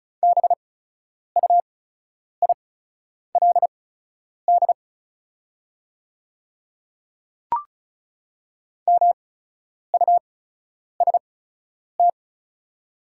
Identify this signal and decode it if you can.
Morse code, heard as a keyed tone, about 36 wpm, spaced as for 10 wpm, 690 Hz. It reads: BUILD MUST
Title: 8x Character Spacing - Top 300 Words - 35wpm